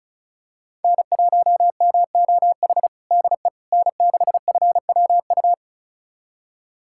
Morse code, heard as a keyed tone, about 36 wpm, 700 Hz. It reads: N1MOH DE N6FWU
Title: USA Callsign DE Callsign - 5 Characters - 35wpm